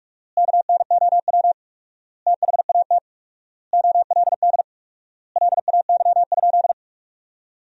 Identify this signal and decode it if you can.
Morse code, heard as a keyed tone, about 46 wpm, 700 Hz. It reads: KNOW THAT OLD LAY?
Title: 4x Word-Spacing - Sentences from Top 400 Words - 45wpm